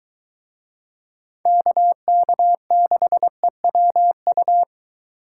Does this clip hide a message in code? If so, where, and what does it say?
Yes: Morse code from 1.5 to 4.6 s, heard as a keyed tone, about 22 wpm, 700 Hz. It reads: KK6EWU